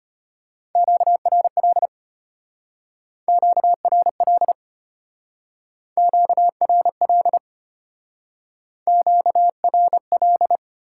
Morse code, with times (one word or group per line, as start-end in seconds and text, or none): 0.8-1.9 QRL
3.3-4.5 QRL
6.0-7.4 QRL
8.9-10.6 QRL